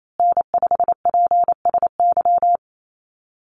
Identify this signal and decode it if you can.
Morse code, heard as a keyed tone, about 28 wpm, 700 Hz. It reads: N5PSY
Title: USA Call Signs Repeated 2x 28wpm